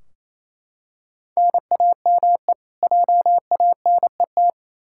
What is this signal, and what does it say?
Morse code, heard as a keyed tone, about 28 wpm, 700 Hz. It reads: NAME JANET
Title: K1USN SST QSO Practice 28wpm